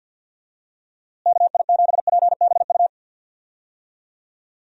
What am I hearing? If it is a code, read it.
Morse code, heard as a keyed tone, about 52 wpm, 700 Hz. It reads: KI7PBU
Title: USA Call Signs - 6 Characters - Repeated 3x - 50wpm